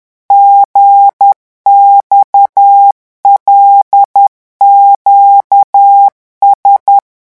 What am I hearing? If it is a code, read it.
Morse code, heard as a keyed tone, about 10 wpm, 800 Hz. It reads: GXLQS
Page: REVISION 18 - 10 mots minute